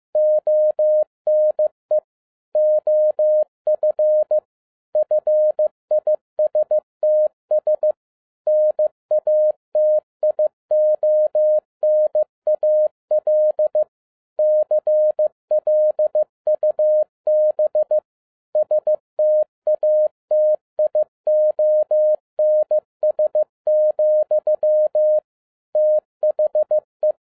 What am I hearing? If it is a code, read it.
Morse code, heard as a keyed tone, about 15 wpm, 610 Hz. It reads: ONE OF FISTS NATIONAL CLUB STATIONS, THE
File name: 2011-10-mp3 / n12-JL3YMV 15wpm